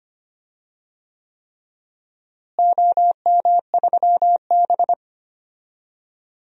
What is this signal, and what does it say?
Morse code, heard as a keyed tone, about 25 wpm, 700 Hz. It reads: OM3B
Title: DXCC-100 Call Signs - 4 Characters - Repeated 3x - 25wpm